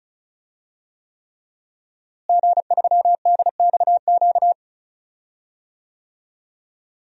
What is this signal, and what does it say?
Morse code, heard as a keyed tone, about 36 wpm, 700 Hz. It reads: G3DXQ
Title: DXCC-100 Call Signs - 5 Characters - Repeated 3x - 35wpm